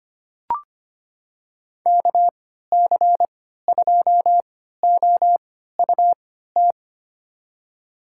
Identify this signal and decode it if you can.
Morse code, heard as a keyed tone, about 25 wpm, spaced as for 14 wpm, 700 Hz. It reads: KC2OUT
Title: USA Call Signs - 6 Characters - 3x Character Spacing - 25wpm